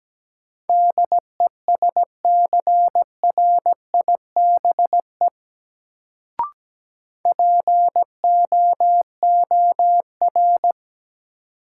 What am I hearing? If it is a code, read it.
Morse code, heard as a keyed tone, about 17 wpm, 700 Hz. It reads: DESCRIBE POOR
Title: Sentences from Top 1000 Words - New Word Drill - 17wpm